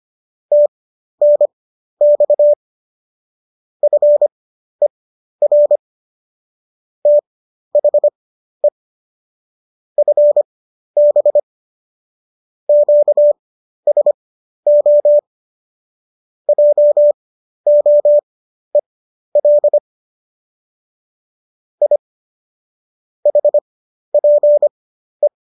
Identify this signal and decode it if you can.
Morse code, heard as a keyed tone, about 25 wpm, spaced as for 12 wpm, 590 Hz. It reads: TNX FER THE FB QSO JOEL I HPE
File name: qso009 12